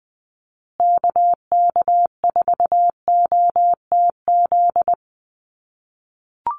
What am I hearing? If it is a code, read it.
Morse code, heard as a keyed tone, about 20 wpm, 700 Hz. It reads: KK4OTZ
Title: USA Call Signs Repeated 3x 20wpm